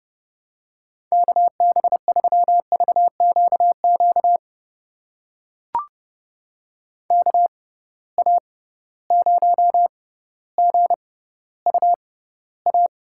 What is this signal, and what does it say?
Morse code, heard as a keyed tone, about 32 wpm, spaced as for 16 wpm, 700 Hz. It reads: KB3VQQ KA0GUA
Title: USA Call Signs - 6 Characters - 6x Character Spacing - 30wpm